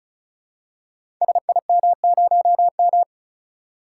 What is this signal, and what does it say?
Morse code, heard as a keyed tone, about 36 wpm, 700 Hz. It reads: SIM0M
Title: Code Groups - 5 Letter:Numbers - 35wpm